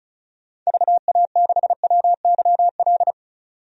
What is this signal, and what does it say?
Morse code, heard as a keyed tone, about 36 wpm, 700 Hz. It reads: VA6WYL
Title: DXCC-100 Call Signs - 6 Characters - 35wpm